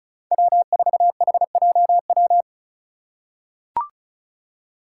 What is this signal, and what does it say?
Morse code, heard as a keyed tone, about 34 wpm, 700 Hz. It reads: W4HJW
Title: Five Challenging Letter-Number Combinations 35wpm